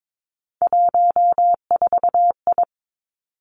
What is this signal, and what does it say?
Morse code, heard as a keyed tone, about 22 wpm, 700 Hz. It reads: 14I
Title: Three Challenging Letter-Number Combinations 22wpm